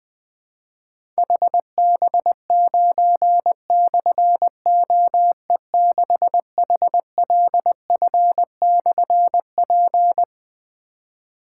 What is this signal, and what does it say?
Morse code, heard as a keyed tone, about 20 wpm, 700 Hz. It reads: HB9/OE6HLF/P
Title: Difficult Call Signs 20wpm